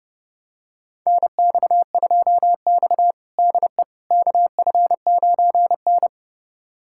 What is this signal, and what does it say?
Morse code, heard as a keyed tone, about 32 wpm, 700 Hz. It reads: NX2X DE KF9N
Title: USA Callsign DE Callsign - 4 Characters - 30wpm